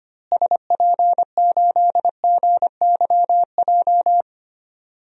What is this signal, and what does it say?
Morse code, heard as a keyed tone, about 25 wpm, 700 Hz. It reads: SP8GYJ
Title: DXCC-100 Call Signs - 6 Characters - Repeated 2x - 25wpm